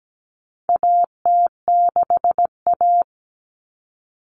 Morse code, heard as a keyed tone, about 17 wpm, 700 Hz. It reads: AT6A
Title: Common Cut Numbers - 4-Digit 17wpm